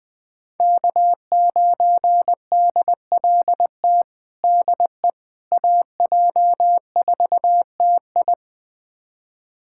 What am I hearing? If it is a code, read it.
Morse code, heard as a keyed tone, about 21 wpm, 700 Hz. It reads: K9DLT DE AJ4TI
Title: USA Callsign DE Callsign - 5 Characters - 20wpm